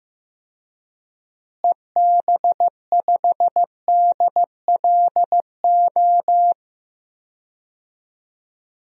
Morse code, heard as a keyed tone, about 15 wpm, 700 Hz. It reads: EB5DLO
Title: DXCC-100 Call Signs - 6 Characters - Repeated 2x - 15wpm